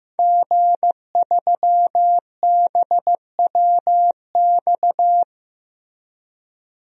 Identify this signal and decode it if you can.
Morse code, heard as a keyed tone, about 15 wpm, 700 Hz. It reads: G3BWX
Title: DXCC-100 Call Signs - 5 Characters - Repeated 3x - 15wpm